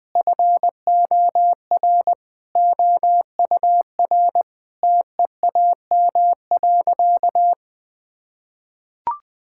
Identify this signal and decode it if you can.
Morse code, heard as a keyed tone, about 21 wpm, 700 Hz. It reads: FOR OUR TEAM.
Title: Sentences with 8-Word Phrases - 20wpm